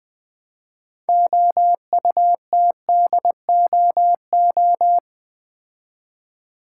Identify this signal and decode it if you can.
Morse code, heard as a keyed tone, about 21 wpm, 700 Hz. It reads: OUTDOO
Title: Build A Word - 2 Letters At A Time - 8-Letter Words - 20wpm